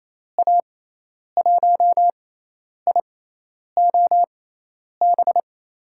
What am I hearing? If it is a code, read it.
Morse code, heard as a keyed tone, about 28 wpm, spaced as for 10 wpm, 700 Hz. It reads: A1IOB
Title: USA Call Signs - 6 Characters - 6x Character Spacing - 28wpm